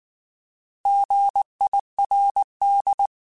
Morse code, heard as a keyed tone, about 19 wpm, 800 Hz. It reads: GIRD